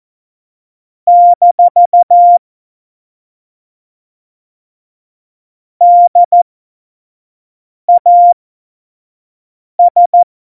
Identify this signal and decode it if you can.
Morse code, heard as a keyed tone, about 14 wpm, spaced as for 5 wpm, 700 Hz. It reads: - DAS
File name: Morsekurs / Witze / Witz 14. Mai 2019, 25 F70